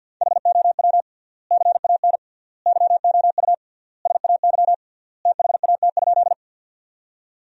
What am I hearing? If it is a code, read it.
Morse code, heard as a keyed tone, about 50 wpm, 700 Hz. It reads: HOW CAN YOU SAY THAT?
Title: 3x Word-Spacing - Sentences from Top 200 Words - 50wpm